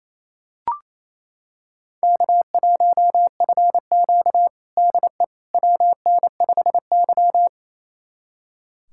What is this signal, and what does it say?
Morse code, heard as a keyed tone, about 28 wpm, 700 Hz. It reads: K1FQ DE WN5Y
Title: USA Callsign DE Callsign - 4 Characters - 28wpm